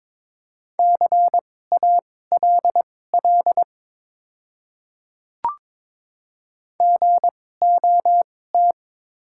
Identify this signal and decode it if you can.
Morse code, heard as a keyed tone, about 22 wpm, spaced as for 16 wpm, 700 Hz. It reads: CALL GOT
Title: 2x Character Spacing - Top 300 Words - 22wpm